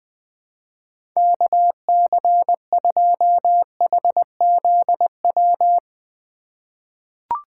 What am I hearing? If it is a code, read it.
Morse code, heard as a keyed tone, about 20 wpm, 700 Hz. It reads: KC2HZW
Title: USA Call Signs - 2x Character Spacing - 20wpm